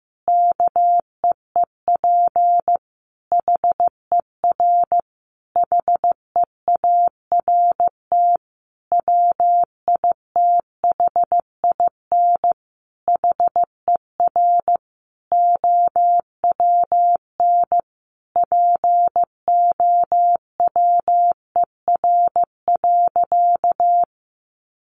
Morse code, heard as a keyed tone, about 15 wpm, 700 Hz. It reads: KEEP HER HEART WITHIN HER OWN POWER.
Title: Sentences from Top 1200 Words 15wpm